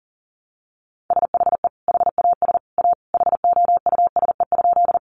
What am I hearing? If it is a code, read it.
Morse code, heard as a keyed tone, about 40 wpm, 700 Hz. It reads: SHE HAS A HOUSE?
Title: Sentences from Top 200 Words 40wpm